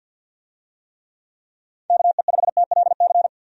Morse code, heard as a keyed tone, about 51 wpm, 700 Hz. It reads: KE5TLC
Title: USA Call Signs - 6 Characters - Repeated 3x - 50wpm